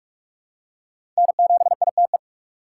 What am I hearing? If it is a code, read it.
Morse code, heard as a keyed tone, about 43 wpm, 700 Hz. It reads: N7ITE
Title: USA Call Signs - 5 Characters - 2x Character Spacing - 45wpm